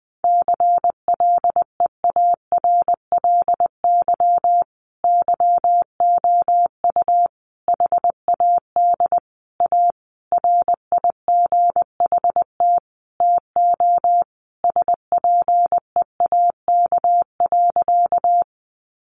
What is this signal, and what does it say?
Morse code, heard as a keyed tone, about 20 wpm, 700 Hz. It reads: CLEARLY YOU HAD A RIGHT TO SPEAK.
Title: Sentences from Top 1300 Words 20wpm